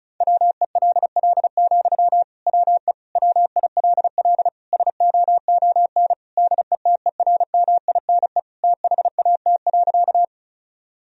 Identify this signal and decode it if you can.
Morse code, heard as a keyed tone, about 34 wpm, 700 Hz. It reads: WELL, WE WILL SOON DETERMINE THAT.
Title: Sentences from Top 1000 Words - Review Part 4 - 35wpm